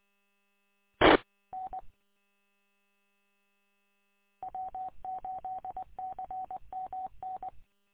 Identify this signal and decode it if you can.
Morse code, heard as a keyed tone, about 20 wpm, 750 Hz. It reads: T N W8CMN
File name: w8cmn- -combined